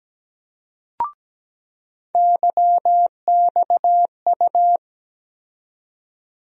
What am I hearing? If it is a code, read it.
Morse code, heard as a keyed tone, about 17 wpm, 700 Hz. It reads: YXU
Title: USA Call Sign Postfixes - 3 Letters 17wpm